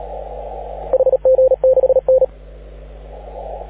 Morse code, heard as a keyed tone, about 37 wpm, 530 Hz. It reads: HG6N
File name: AUDIO2019 / HG6N CW SSB MIX